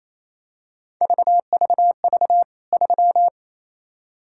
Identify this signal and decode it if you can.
Morse code, heard as a keyed tone, about 28 wpm, 700 Hz. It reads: VVV 3
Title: K1USN SST QSO Practice 28wpm